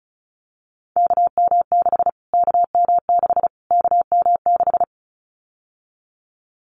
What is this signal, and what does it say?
Morse code, heard as a keyed tone, about 34 wpm, 700 Hz. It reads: KM6 KM6 KM6
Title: Callsign Prefixes for Top 25 DXCC - Repeated 3x - 35wpm